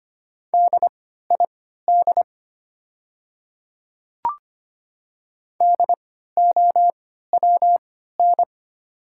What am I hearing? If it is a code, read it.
Morse code, heard as a keyed tone, about 25 wpm, spaced as for 14 wpm, 700 Hz. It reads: DID DOWN
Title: 3x Character Spacing - Top 300 Words - 25wpm